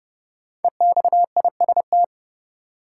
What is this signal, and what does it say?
Morse code, heard as a keyed tone, about 32 wpm, 700 Hz. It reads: EXIST
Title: Sentences from Top 1000 Words - Review Part 5 - New Word Drill - 30wpm